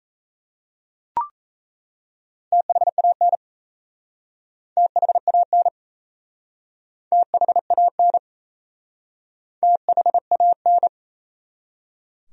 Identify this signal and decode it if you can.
Morse code, heard as a keyed tone, about 38 wpm, 700 Hz. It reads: THAN THAN THAN THAN